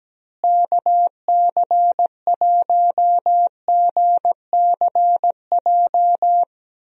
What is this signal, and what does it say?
Morse code, heard as a keyed tone, about 17 wpm, 700 Hz. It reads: KC1GCJ
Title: USA Call Signs - 6 Characters - Repeated 3x - 17wpm